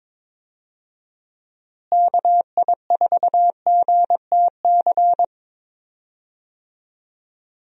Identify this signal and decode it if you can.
Morse code, heard as a keyed tone, about 22 wpm, 700 Hz. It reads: KI4GTC
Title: USA Call Signs - 6 Characters - Repeated 2x - 22wpm